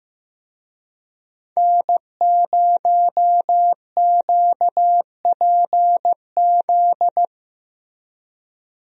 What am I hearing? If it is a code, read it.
Morse code, heard as a keyed tone, about 15 wpm, 700 Hz. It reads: N0QPZ
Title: USA Call Signs - 5 Characters - Repeated 3x - 15wpm